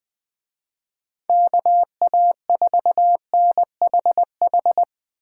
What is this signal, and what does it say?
Morse code, heard as a keyed tone, about 20 wpm, 700 Hz. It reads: KA4NHH